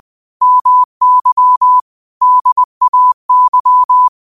Morse code, heard as a keyed tone, about 20 wpm, 1000 Hz. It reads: MY DAY
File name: qotdfiles / 20wpm